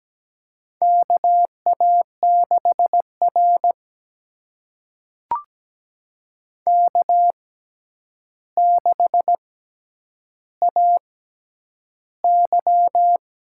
Morse code, heard as a keyed tone, about 17 wpm, spaced as for 9 wpm, 700 Hz. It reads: KA6R K6AY
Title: USA Call Signs - 4 Characters - 6x Character Spacing - 17wpm